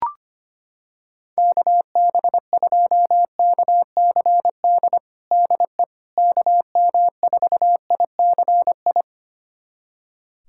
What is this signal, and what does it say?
Morse code, heard as a keyed tone, about 25 wpm, 700 Hz. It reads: KB2KCD DE KM4ICI
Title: USA Callsign DE Callsign 25wpm